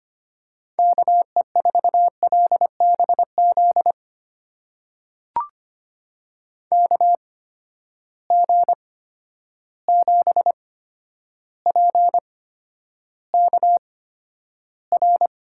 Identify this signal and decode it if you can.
Morse code, heard as a keyed tone, about 25 wpm, spaced as for 11 wpm, 700 Hz. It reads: KE4LBZ KG7PKR
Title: USA Call Signs - 6 Characters - 8x Character Spacing - 25wpm